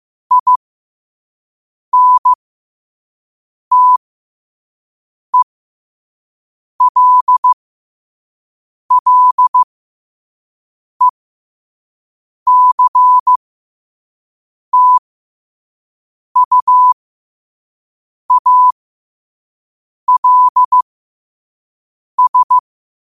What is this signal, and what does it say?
Morse code, heard as a keyed tone, about 15 wpm, spaced as for 5 wpm, 1000 Hz. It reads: INTELLECTUALS
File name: qotdfiles / 5wpm